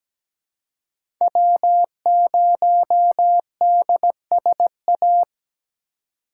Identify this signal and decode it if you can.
Morse code, heard as a keyed tone, about 17 wpm, 700 Hz. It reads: W0DSA
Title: USA Call Signs - 3x Character Spacing - 17wpm